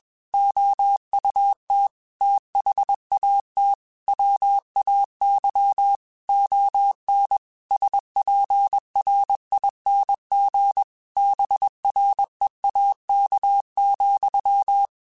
Morse code, heard as a keyed tone, about 21 wpm, 800 Hz. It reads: OUT THAT WAY ON SPRING BREAK,
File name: QSOs / QSOs-mp3s / WA7QOX 1@20